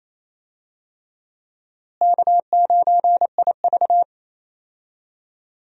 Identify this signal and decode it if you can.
Morse code, heard as a keyed tone, about 28 wpm, 700 Hz. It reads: K9IV
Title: USA Call Signs - 4 Characters - Repeated 3x - 28wpm